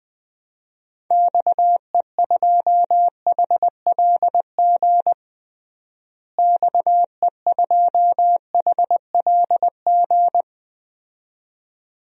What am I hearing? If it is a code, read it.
Morse code, heard as a keyed tone, about 20 wpm, 700 Hz. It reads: XE2HLG XE2HLG
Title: DXCC-100 Call Signs - 6 Characters - Repeated 2x - 20wpm